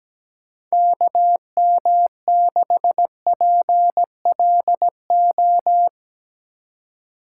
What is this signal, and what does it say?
Morse code, heard as a keyed tone, about 17 wpm, 700 Hz. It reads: KM6PLO
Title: USA Call Signs - 6 Characters - 2x Character Spacing - 17wpm